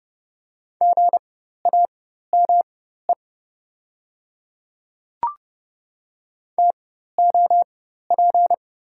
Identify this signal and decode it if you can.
Morse code, heard as a keyed tone, about 32 wpm, spaced as for 14 wpm, 700 Hz. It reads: GAME TOP
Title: 4x Character Spacing - Top 500 Words - 30wpm